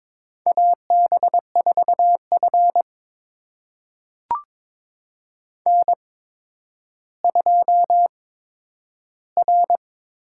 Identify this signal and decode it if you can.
Morse code, heard as a keyed tone, about 22 wpm, spaced as for 11 wpm, 700 Hz. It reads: AB4F N2R
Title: USA Call Signs - 4 Characters - 8x Character Spacing - 22wpm